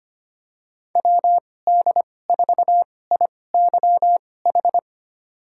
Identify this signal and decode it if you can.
Morse code, heard as a keyed tone, about 25 wpm, spaced as for 18 wpm, 700 Hz. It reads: WD4IYH
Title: USA Call Signs - 2x Character Spacing - 25wpm